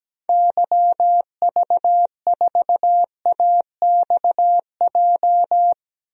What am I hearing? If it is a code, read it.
Morse code, heard as a keyed tone, about 17 wpm, 700 Hz. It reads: YV4AXJ
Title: DXCC-100 Call Signs - 6 Characters - 17wpm